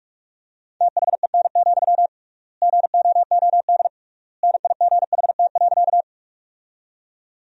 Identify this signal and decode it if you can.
Morse code, heard as a keyed tone, about 47 wpm, 700 Hz. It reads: THEN, GOOD NIGHT.
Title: 3x Word-Spacing - Sentences from Top 300 Words - 45wpm